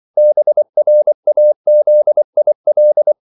Morse code, heard as a keyed tone, about 24 wpm, 600 Hz. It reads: BRAZIL
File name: CW / Practice / NationalCapitals24